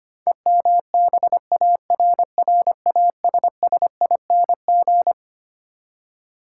Morse code, heard as a keyed tone, about 25 wpm, 700 Hz. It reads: EMBARRASSING
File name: Twelve Letter Words - 25wpm